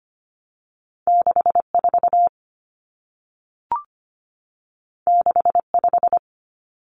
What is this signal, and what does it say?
Morse code, heard as a keyed tone, about 25 wpm, 700 Hz. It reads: 64 65
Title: Common Cut Numbers - 2-Digit 25wpm